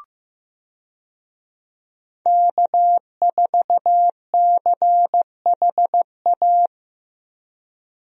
Morse code, heard as a keyed tone, about 15 wpm, 700 Hz. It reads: K4CHA
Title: USA Call Signs - 5 Characters - Repeated 2x - 15wpm